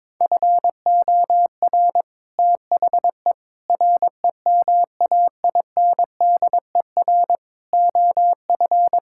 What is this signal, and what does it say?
Morse code, heard as a keyed tone, about 22 wpm, 700 Hz. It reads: FOR THE REMAINDER OF
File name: practice / 22wpm / Sets of 6 Words - 22wpm